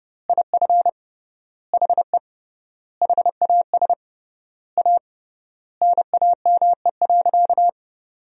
Morse code, heard as a keyed tone, about 30 wpm, 700 Hz. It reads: IF HE HAS A NAME.
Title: 3x Word-Spacing - Sentences from Top 500 Words - 30wpm